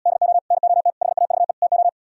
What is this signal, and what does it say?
Morse code, heard as a keyed tone, about 77 wpm, 700 Hz. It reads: DO NOT HAVE TO